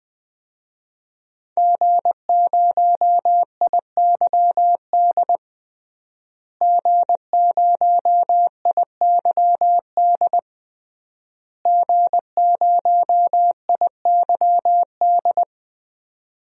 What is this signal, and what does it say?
Morse code, heard as a keyed tone, about 21 wpm, 700 Hz. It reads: G0IYD G0IYD G0IYD
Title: DXCC-100 Call Signs - 5 Characters - Repeated 3x - 20wpm